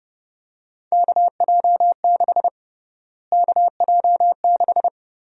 Morse code, heard as a keyed tone, about 30 wpm, 700 Hz. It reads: KJ6 KJ6
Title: USA Call Sign Prefixes - Repeated 2x - 30wpm